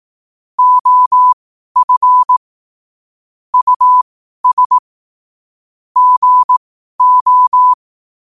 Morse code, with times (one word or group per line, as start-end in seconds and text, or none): 0.6-2.4 OF
3.5-4.8 US
6.0-7.7 GO